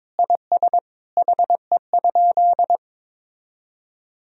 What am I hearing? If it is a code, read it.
Morse code, heard as a keyed tone, about 22 wpm, 700 Hz. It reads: IS HE?
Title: Sentences from Top 1000 Words - Review Part 4 - 22wpm